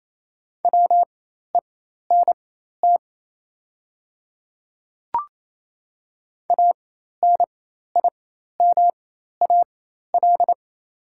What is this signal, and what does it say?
Morse code, heard as a keyed tone, about 28 wpm, spaced as for 13 wpm, 690 Hz. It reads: WENT ANIMAL
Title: 4x Character Spacing - Top 300 Words - 28wpm